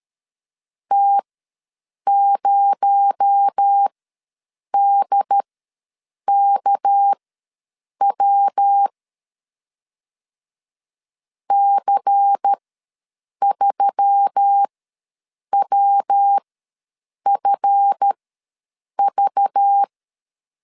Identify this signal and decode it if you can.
Morse code, heard as a keyed tone, about 13 wpm, spaced as for 7 wpm, 800 Hz. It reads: T0DKW C3WFV
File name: LEZIONI / 7 GF9Z / 7 GF ws 12wpm 3140